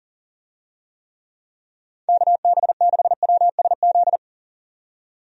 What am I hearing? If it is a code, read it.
Morse code, heard as a keyed tone, about 40 wpm, 700 Hz. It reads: KB6WSZ
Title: USA Call Signs - 6 Characters - 40wpm